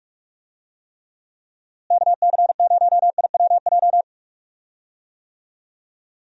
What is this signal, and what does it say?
Morse code, heard as a keyed tone, about 46 wpm, 700 Hz. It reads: KC0IWJ
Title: USA Call Signs - 6 Characters - Repeated 3x - 45wpm